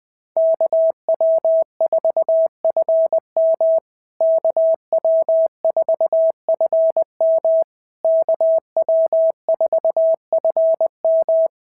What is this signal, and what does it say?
Morse code, heard as a keyed tone, about 20 wpm, 650 Hz. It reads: KW4FM KW4FM KW4FM
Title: POTA QSO Practice - 20wpm